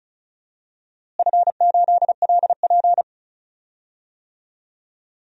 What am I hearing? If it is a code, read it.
Morse code, heard as a keyed tone, about 34 wpm, 700 Hz. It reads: F8LP
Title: DXCC-100 Call Signs - 4 Characters - Repeated 3x - 35wpm